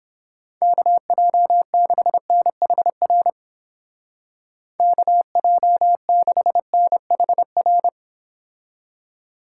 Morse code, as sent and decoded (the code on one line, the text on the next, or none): -.- .--- -.... -. .... .-. / -.- .--- -.... -. .... .-.
KJ6NHR KJ6NHR